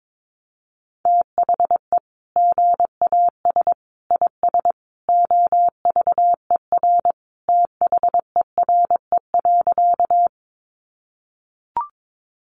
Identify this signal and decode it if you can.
Morse code, heard as a keyed tone, about 22 wpm, 700 Hz. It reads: THE GAS IS OVER THERE.
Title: Sentences from Top 1100 Words 22wpm